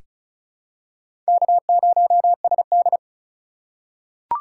Morse code, heard as a keyed tone, about 36 wpm, 700 Hz. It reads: K0SD
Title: USA Call Signs - 4x Character Spacing - 35wpm